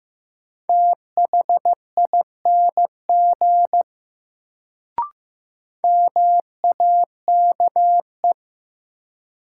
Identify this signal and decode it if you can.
Morse code, heard as a keyed tone, about 15 wpm, 700 Hz. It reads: THING MAKE
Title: Sentences from Top 100 Words - New Word Drill - 15wpm